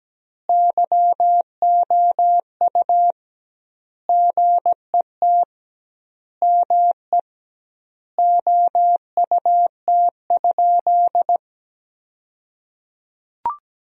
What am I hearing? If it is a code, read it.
Morse code, heard as a keyed tone, about 17 wpm, 700 Hz. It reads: YOU GET ME OUT?
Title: 2x Word-Spacing - Sentences from Top 200 Words - 17wpm